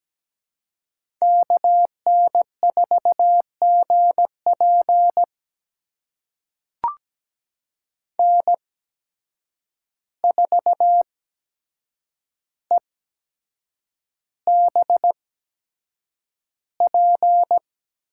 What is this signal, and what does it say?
Morse code, heard as a keyed tone, about 17 wpm, spaced as for 7 wpm, 700 Hz. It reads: KN4GP N4EBP
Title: USA Call Signs - 5 Characters - 8x Character Spacing - 17wpm